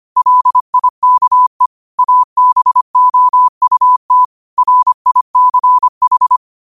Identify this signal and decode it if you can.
Morse code, heard as a keyed tone, about 25 wpm, 1000 Hz. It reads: LIKE ABOUT RICH
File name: qotdfiles / 25wpm